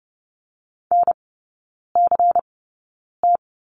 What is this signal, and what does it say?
Morse code, heard as a keyed tone, about 30 wpm, spaced as for 9 wpm, 700 Hz. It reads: NCT
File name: Lesson 16e - Letter c - Three Characters 30wpm